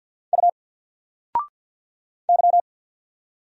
Morse code, heard as a keyed tone, about 49 wpm, 700 Hz. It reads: U Y
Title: Single Character (Letter:Number:Punctuation:Prosign) 50wpm